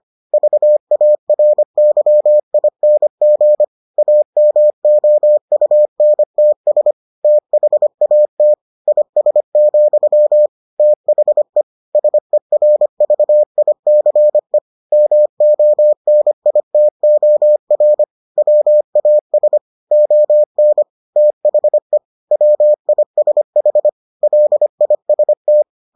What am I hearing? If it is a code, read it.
Morse code, heard as a keyed tone, about 25 wpm, 600 Hz. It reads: VARYING AMOUNTS THAT IS, THE SERVICE MONITOR WAS ON THE WISH LIST